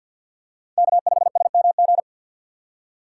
Morse code, heard as a keyed tone, about 52 wpm, 700 Hz. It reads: K5SMG
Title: USA Call Signs - 5 Characters - 3x Character Spacing - 50wpm